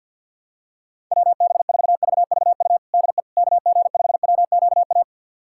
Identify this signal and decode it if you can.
Morse code, heard as a keyed tone, about 52 wpm, 690 Hz. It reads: WB4VVU DE KG5WQA
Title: USA Callsign DE Callsign - 6 Characters - 50wpm